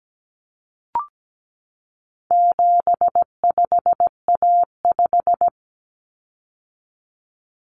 Morse code, heard as a keyed tone, about 17 wpm, 700 Hz. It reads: 75A5
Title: Common Cut Numbers - 4-Digit 17wpm